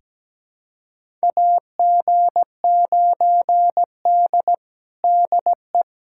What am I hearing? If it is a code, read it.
Morse code, heard as a keyed tone, about 17 wpm, 700 Hz. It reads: AG9D DE
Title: USA Callsign DE Callsign - 4 Characters - 17wpm